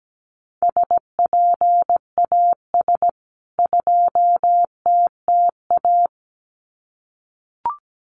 Morse code, heard as a keyed tone, about 17 wpm, 700 Hz. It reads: SPAS 2TTA
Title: CWT Exchange 17wpm